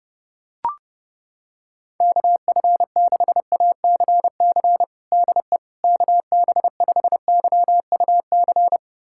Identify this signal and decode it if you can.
Morse code, heard as a keyed tone, about 29 wpm, 700 Hz. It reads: KF6ACC DE KB5YUC
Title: USA Callsign DE Callsign 30wpm